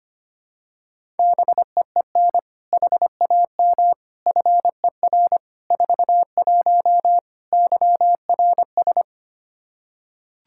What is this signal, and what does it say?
Morse code, heard as a keyed tone, about 25 wpm, 700 Hz. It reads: BEEN HAM FER 41 YRS
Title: Practice QSO - Exchange 3 - More Info - 25wpm